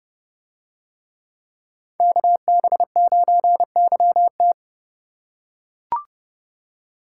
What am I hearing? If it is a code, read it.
Morse code, heard as a keyed tone, about 28 wpm, 700 Hz. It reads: KB9YT